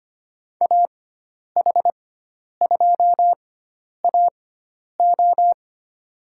Morse code, heard as a keyed tone, about 25 wpm, spaced as for 10 wpm, 700 Hz. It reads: AH2AO
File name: Lesson 16g - Letter c - Call Signs 25wpm